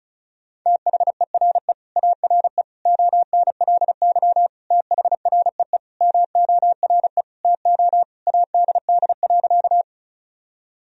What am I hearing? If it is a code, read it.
Morse code, heard as a keyed tone, about 34 wpm, 700 Hz. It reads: THERE ARE ONLY THREE MORE TO ADD.
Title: Sentences from Top 1000 Words - Review Part 1 - 35wpm